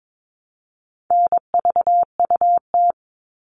Morse code, heard as a keyed tone, about 22 wpm, 700 Hz. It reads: NVUT
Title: All Cut Numbers - 4-Digit 22wpm